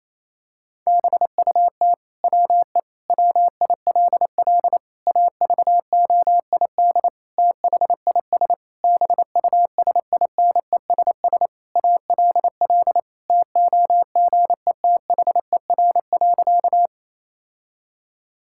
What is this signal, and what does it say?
Morse code, heard as a keyed tone, about 28 wpm, 700 Hz. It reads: BUT WE WILL AVOID THIS BUSINESS ALL TOGETHER.
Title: Sentences from Top 1000 Words - Review Part 6 - 28wpm